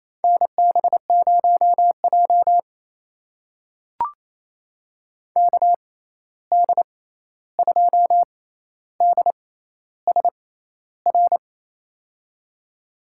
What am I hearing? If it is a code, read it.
Morse code, heard as a keyed tone, about 28 wpm, spaced as for 13 wpm, 700 Hz. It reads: NB0J KD2DSR
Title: USA Call Signs - 6x Character Spacing - 28wpm